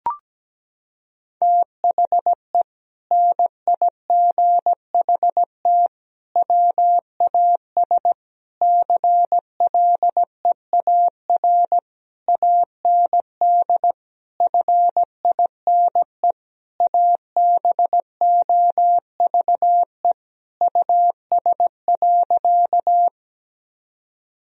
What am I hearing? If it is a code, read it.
Morse code, heard as a keyed tone, about 17 wpm, 700 Hz. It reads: THE NIGHT WAS CLEAR AND FINE ABOVE US.
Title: Sentences from Top 1000 Words - Review Part 2 - 17wpm